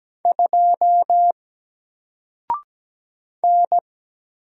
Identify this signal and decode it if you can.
Morse code, heard as a keyed tone, about 17 wpm, 700 Hz. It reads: 2 N
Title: Single Letter-Number Alt 17wpm